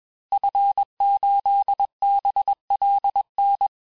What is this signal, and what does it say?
Morse code, heard as a keyed tone, about 21 wpm, 800 Hz. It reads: F8BLN
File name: teste-FIN-de-QSO-vitesse-20mpm